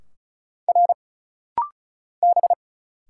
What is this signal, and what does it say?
Morse code, heard as a keyed tone, about 36 wpm, 700 Hz. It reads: R B
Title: Single Letter-Number Alt - Rapid-Fire 35wpm